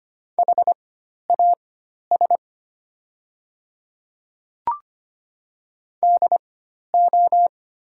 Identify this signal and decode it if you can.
Morse code, heard as a keyed tone, about 25 wpm, spaced as for 12 wpm, 700 Hz. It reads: HAS DO
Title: 4x Character Spacing - Top 100 Words - 25wpm